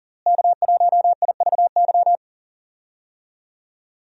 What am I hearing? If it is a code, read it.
Morse code, heard as a keyed tone, about 40 wpm, 700 Hz. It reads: K1IVY
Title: USA Call Signs - 5 Characters - Repeated 2x - 40wpm